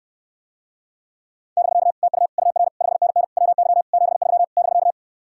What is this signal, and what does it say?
Morse code, heard as a keyed tone, about 69 wpm, 700 Hz. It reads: <BK> TU UR 5NN WY 73 <BK>